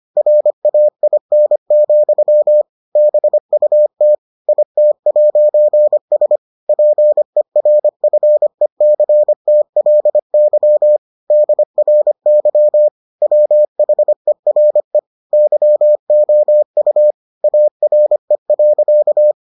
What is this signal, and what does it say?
Morse code, heard as a keyed tone, about 25 wpm, 600 Hz. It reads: RAIN, BUT IT'S PERFECTLY DRY WHERE YOU ARE.